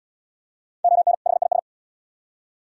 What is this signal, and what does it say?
Morse code, heard as a keyed tone, about 76 wpm, 700 Hz. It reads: ON HIS